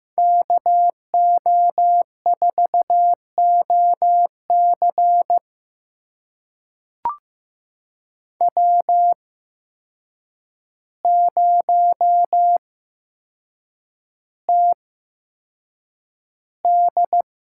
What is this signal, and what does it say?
Morse code, heard as a keyed tone, about 15 wpm, spaced as for 7 wpm, 700 Hz. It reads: KO4OC W0TD
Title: USA Call Signs - 5 Characters - 8x Character Spacing - 15wpm